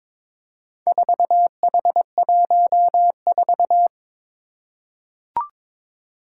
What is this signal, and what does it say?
Morse code, heard as a keyed tone, about 22 wpm, 700 Hz. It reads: 4H14
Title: Four Challenging Letter-Number Combinations 22wpm